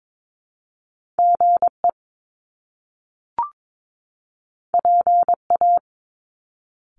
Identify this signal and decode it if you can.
Morse code, heard as a keyed tone, about 22 wpm, 700 Hz. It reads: GE PA
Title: Top Two Letter Combinations in Words 22wpm